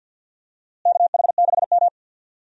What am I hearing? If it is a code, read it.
Morse code, heard as a keyed tone, about 52 wpm, 700 Hz. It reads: KH6M
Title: USA Call Signs - 4 Characters - Repeated 2x - 50wpm